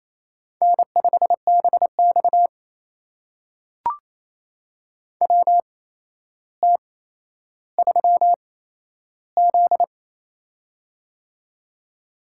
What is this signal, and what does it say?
Morse code, heard as a keyed tone, about 28 wpm, spaced as for 12 wpm, 700 Hz. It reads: N5BX WT3Z
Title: USA Call Signs - 4 Characters - 8x Character Spacing - 28wpm